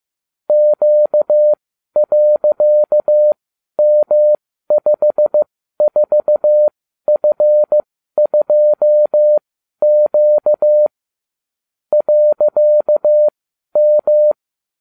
Morse code, heard as a keyed tone, about 15 wpm, spaced as for 12 wpm, 600 Hz. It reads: Q.M54F2Q .M
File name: cw / ubacw34 3